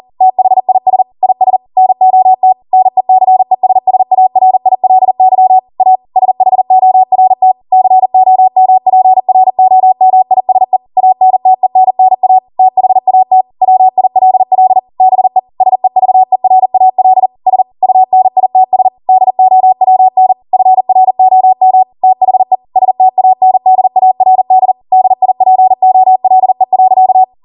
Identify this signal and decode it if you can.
Morse code, heard as a keyed tone, about 40 wpm, 750 Hz. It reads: THIS IS NOT NECESSARILY A SHORT COMPROMISE ANTENNA THAT WILL BE SEVERAL S UNITS DOWN FROM THE STANDARD DIPOLE.